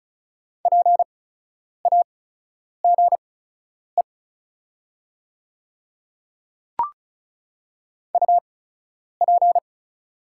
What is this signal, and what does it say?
Morse code, heard as a keyed tone, about 36 wpm, spaced as for 10 wpm, 690 Hz. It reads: PAGE UP
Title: 8x Character Spacing - Top 300 Words - 35wpm